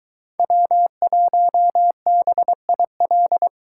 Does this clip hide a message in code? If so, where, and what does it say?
Yes: Morse code from 0.4 to 3.5 s, heard as a keyed tone, about 23 wpm, 700 Hz. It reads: W1BIL